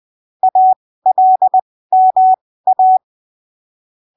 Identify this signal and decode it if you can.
Morse code, heard as a keyed tone, about 20 wpm, spaced as for 15 wpm, 750 Hz. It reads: ALMA
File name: arquivos / texto / texto 29 20ppm